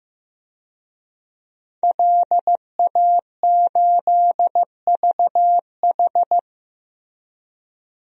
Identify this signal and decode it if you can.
Morse code, heard as a keyed tone, about 15 wpm, 700 Hz. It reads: LA8VH
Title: DXCC-100 Call Signs - 5 Characters - Repeated 2x - 15wpm